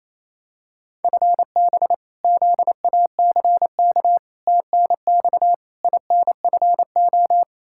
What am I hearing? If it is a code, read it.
Morse code, heard as a keyed tone, about 28 wpm, 700 Hz. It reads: FB ZACK TNX INFO
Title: Practice QSO - Exchange 3 - More Info - 28wpm